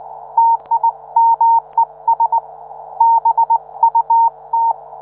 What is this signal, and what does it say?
Morse code, heard as a keyed tone, about 19 wpm, 920 Hz. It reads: TIMES BUT